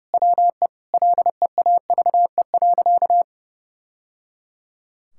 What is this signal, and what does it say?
Morse code, heard as a keyed tone, about 30 wpm, 700 Hz. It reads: WE LEAVE.
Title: Sentences from Top 1000 Words - Review Part 5 - 30wpm